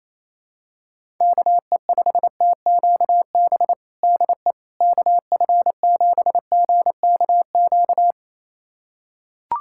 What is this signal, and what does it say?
Morse code, heard as a keyed tone, about 27 wpm, 700 Hz. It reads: KE5TQB DE KF7GKQ
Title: USA Callsign DE Callsign - 6 Characters - 28wpm